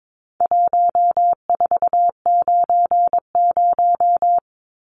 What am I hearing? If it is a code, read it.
Morse code, heard as a keyed tone, about 22 wpm, 700 Hz. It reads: 1490
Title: Numbers - 4-Digit 22wpm